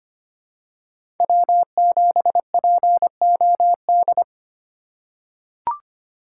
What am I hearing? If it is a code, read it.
Morse code, heard as a keyed tone, about 25 wpm, 700 Hz. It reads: W7POD
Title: USA Call Signs - 5 Characters - 3x Character Spacing - 25wpm